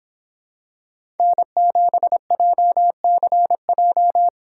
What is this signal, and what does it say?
Morse code, heard as a keyed tone, about 26 wpm, 700 Hz. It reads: N7JCJ